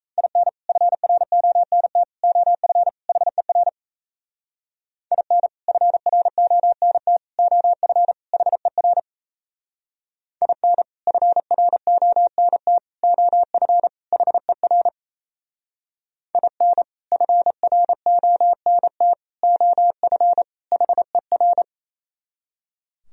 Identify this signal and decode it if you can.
Morse code, heard as a keyed tone, about 35 wpm, 690 Hz. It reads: IN FRONT OF HER IN FRONT OF HER IN FRONT OF HER IN FRONT OF HER